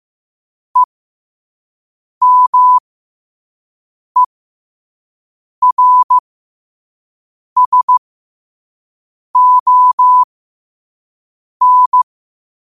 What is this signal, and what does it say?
Morse code, heard as a keyed tone, about 15 wpm, spaced as for 5 wpm, 1000 Hz. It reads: EMERSON
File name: qotdfiles / 5wpm